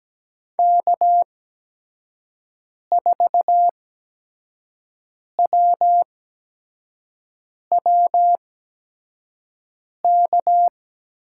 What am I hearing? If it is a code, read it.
Morse code, heard as a keyed tone, about 17 wpm, spaced as for 5 wpm, 700 Hz. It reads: K4WWK
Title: USA Call Signs - 5 Characters - 8x Character Spacing - 17wpm